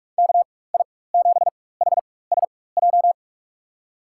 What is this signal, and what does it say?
Morse code, heard as a keyed tone, about 47 wpm, spaced as for 21 wpm, 700 Hz. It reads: KI7HSJ
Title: USA Call Signs - 4x Character Spacing - 45wpm